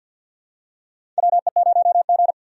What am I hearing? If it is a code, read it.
Morse code, heard as a keyed tone, about 47 wpm, 700 Hz. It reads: WE0G
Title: USA Call Signs - 4 Characters - 50wpm